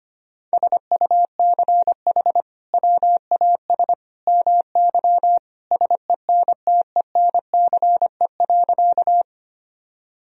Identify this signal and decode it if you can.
Morse code, heard as a keyed tone, about 25 wpm, 700 Hz. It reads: SUCH WAS MY SENTENCE.
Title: Sentences from Top 1000 Words - Review Part 1 - 25wpm